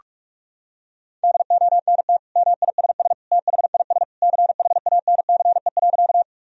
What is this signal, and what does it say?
Morse code, heard as a keyed tone, about 46 wpm, 700 Hz. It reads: DONT MISS THIS CHANCE.
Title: Sentences from Top 1000 Words - Review Part 3 - 45wpm